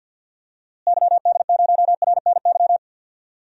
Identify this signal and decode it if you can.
Morse code, heard as a keyed tone, about 51 wpm, 700 Hz. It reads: YD0RNY
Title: DXCC-100 Call Signs - 6 Characters - Repeated 3x - 50wpm